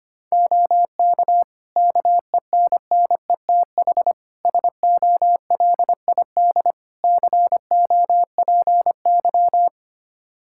Morse code, heard as a keyed tone, about 25 wpm, 700 Hz. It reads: OK KENNETH SOLID COPY
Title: Practice QSO - Exchange 3 - More Info - 25wpm